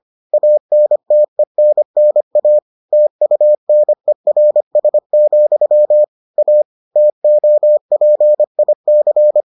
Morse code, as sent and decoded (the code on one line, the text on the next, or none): .- -. - . -. -. .- / - ..- -. .-. ... --..-- / .- / - --- .--. .. -.-.
ANTENNA TUNERS, A TOPIC